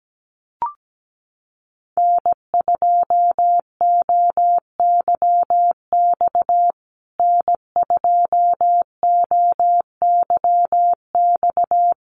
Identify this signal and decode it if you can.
Morse code, heard as a keyed tone, about 17 wpm, 700 Hz. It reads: N2OYX N2OYX
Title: USA Call Signs Repeated 3x 17wpm